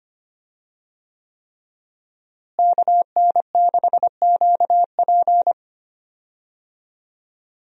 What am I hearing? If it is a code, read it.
Morse code, heard as a keyed tone, about 25 wpm, 700 Hz. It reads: KN6QP
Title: USA Call Signs - 5 Characters - Repeated 3x - 25wpm